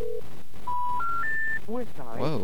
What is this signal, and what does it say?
Special information tone, with speech.